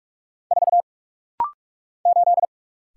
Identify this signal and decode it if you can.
Morse code, heard as a keyed tone, about 46 wpm, 700 Hz. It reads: 4 8
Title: Single Letter-Number - Rapid-Fire 45wpm